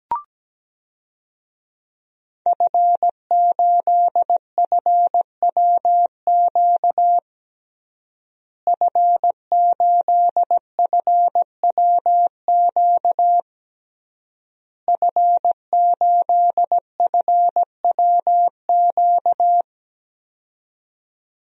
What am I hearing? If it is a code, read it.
Morse code, heard as a keyed tone, about 17 wpm, 700 Hz. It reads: F8FWQ F8FWQ F8FWQ
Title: DXCC-100 Call Signs - 5 Characters - Repeated 3x - 17wpm